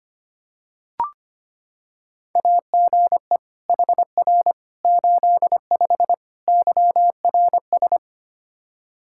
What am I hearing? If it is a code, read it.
Morse code, heard as a keyed tone, about 25 wpm, 700 Hz. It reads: AGE HR 85 YRS
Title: Practice QSO - Exchange 3 - More Info - 25wpm